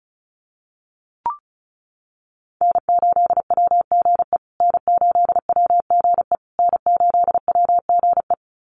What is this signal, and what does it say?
Morse code, heard as a keyed tone, about 34 wpm, 700 Hz. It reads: N8WGE N8WGE N8WGE
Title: USA Call Signs Repeated 3x 35wpm